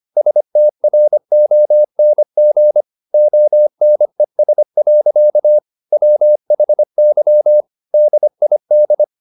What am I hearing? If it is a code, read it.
Morse code, heard as a keyed tone, about 25 wpm, 600 Hz. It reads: STRONG ONES. WHY DID
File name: the old man and the sea part 1 25wpm